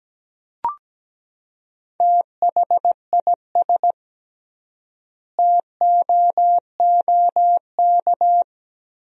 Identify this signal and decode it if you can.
Morse code, heard as a keyed tone, about 17 wpm, 700 Hz. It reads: THIS TOOK
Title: 3x Word-Spacing - Sentences from Top 300 Words - 17wpm